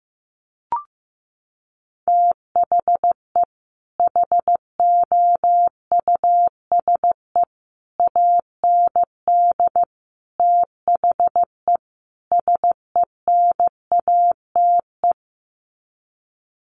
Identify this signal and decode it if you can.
Morse code, heard as a keyed tone, about 14 wpm, 700 Hz. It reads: THE HOUSE AND THE SENATE
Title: Sets of 5 Words Encore2 15wpm